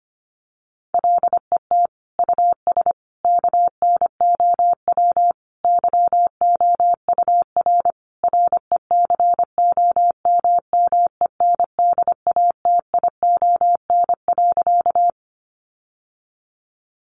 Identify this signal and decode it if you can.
Morse code, heard as a keyed tone, about 25 wpm, 700 Hz. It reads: LET US KNOW YOUR RECOMMENDATION.
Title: Sentences from Top 1400 Words 25wpm